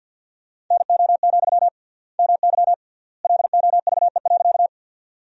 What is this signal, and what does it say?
Morse code, heard as a keyed tone, about 53 wpm, 700 Hz. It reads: NO, MY LOVE.
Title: 3x Word-Spacing - Sentences from Top 400 Words - 50wpm